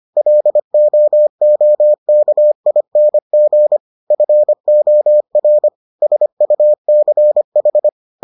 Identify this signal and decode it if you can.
Morse code, heard as a keyed tone, about 25 wpm, 600 Hz. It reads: LOOKING FOR SUCH